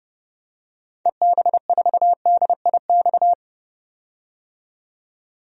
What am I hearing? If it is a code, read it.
Morse code, heard as a keyed tone, about 30 wpm, 700 Hz. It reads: EB4DIX
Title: DXCC-100 Call Signs - 6 Characters - 30wpm